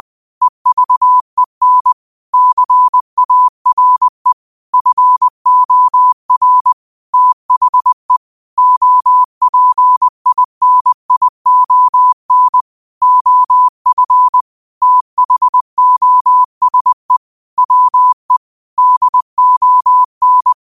Morse code, heard as a keyed tone, about 20 wpm, 1000 Hz. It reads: EVEN CARE FOR THE OPINION OF THOSE WE DON